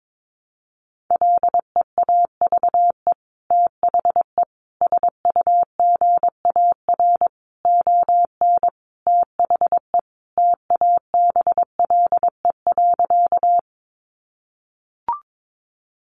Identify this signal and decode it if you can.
Morse code, heard as a keyed tone, about 22 wpm, 700 Hz. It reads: LEAVE THE SUGAR ON THE TABLE.
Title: Sentences from Top 1200 Words 22wpm